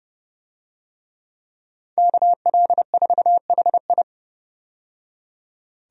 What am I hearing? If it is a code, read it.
Morse code, heard as a keyed tone, about 29 wpm, 700 Hz. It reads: KL4HI
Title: USA Call Signs - 5 Characters - 30wpm